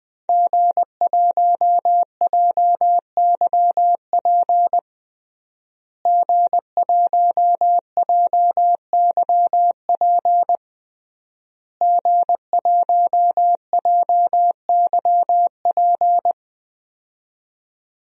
Morse code, heard as a keyed tone, about 20 wpm, 700 Hz. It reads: G1JYP G1JYP G1JYP
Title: DXCC-100 Call Signs - 5 Characters - Repeated 3x - 20wpm